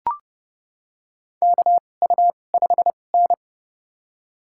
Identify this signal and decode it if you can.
Morse code, heard as a keyed tone, about 30 wpm, spaced as for 22 wpm, 700 Hz. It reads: KU5N
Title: USA Call Signs - 4 Characters - 2x Character Spacing - 30wpm